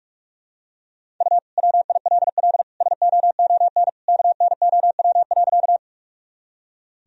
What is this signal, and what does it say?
Morse code, heard as a keyed tone, about 45 wpm, 700 Hz. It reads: U WILL SOON KNOW.
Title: Sentences from Top 400 Words and Abbreviations 45wpm